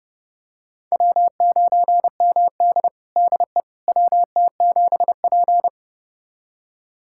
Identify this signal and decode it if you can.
Morse code, heard as a keyed tone, about 32 wpm, 700 Hz. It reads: W9MD DE WT7P
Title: USA Callsign DE Callsign - 4 Characters - 30wpm